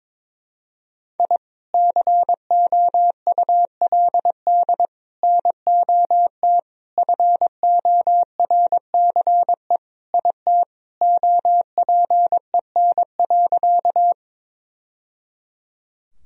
Morse code, heard as a keyed tone, about 22 wpm, 700 Hz. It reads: I COULD NOT FORCE IT OPEN.
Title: Sentences from Top 1000 Words - Review Part 3 - 22wpm